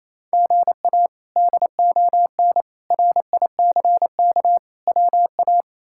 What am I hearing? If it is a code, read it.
Morse code, heard as a keyed tone, about 28 wpm, 700 Hz. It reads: GA DON RICK WA
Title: K1USN SST QSO Practice - 28wpm